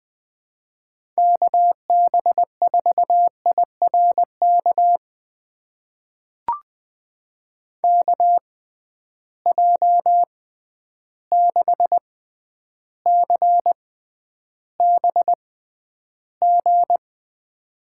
Morse code, heard as a keyed tone, about 21 wpm, spaced as for 10 wpm, 700 Hz. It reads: KB4IRK KJ6CBG
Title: USA Call Signs - 6 Characters - 6x Character Spacing - 20wpm